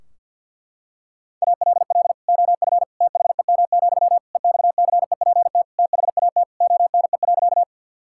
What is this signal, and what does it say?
Morse code, heard as a keyed tone, about 50 wpm, 700 Hz. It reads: ALL OF THEM, EXCEPT THAT ONE.
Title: Sentences from Top 1600 Words 50wpm